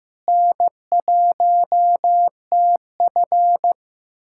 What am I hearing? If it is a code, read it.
Morse code, heard as a keyed tone, about 16 wpm, 700 Hz. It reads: N1TF
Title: USA Callsign DE Callsign - 4 Characters - 15wpm